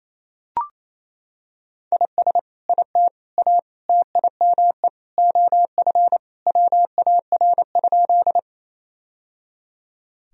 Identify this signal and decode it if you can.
Morse code, heard as a keyed tone, about 28 wpm, 700 Hz. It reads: IS IT A TIME OF WAR?
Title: Sentences from Top 1000 Words - Review Part 2 - 28wpm